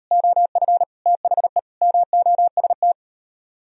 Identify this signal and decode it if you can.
Morse code, heard as a keyed tone, about 37 wpm, 700 Hz. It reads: OF THE MOST